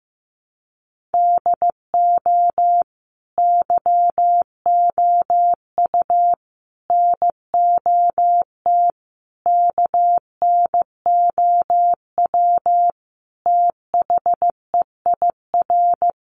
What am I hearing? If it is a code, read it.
Morse code, heard as a keyed tone, about 15 wpm, 700 Hz. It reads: DO YOU NOT KNOW THEIR
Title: Sentences from Top 600 Words 15wpm